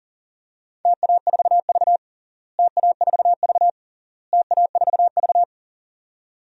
Morse code, heard as a keyed tone, about 40 wpm, 700 Hz. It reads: TA4V TA4V TA4V
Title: DXCC-100 Call Signs - 4 Characters - Repeated 3x - 40wpm